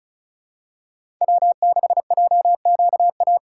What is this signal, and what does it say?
Morse code, heard as a keyed tone, about 36 wpm, 700 Hz. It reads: W6JQA
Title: USA Call Signs - 5 Characters - 3x Character Spacing - 35wpm